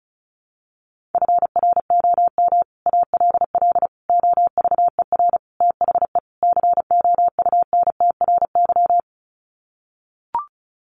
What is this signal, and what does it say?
Morse code, heard as a keyed tone, about 34 wpm, 700 Hz. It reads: FROM ALL OVER THE COUNTRY
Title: Sets of 5 Words Encore 35wpm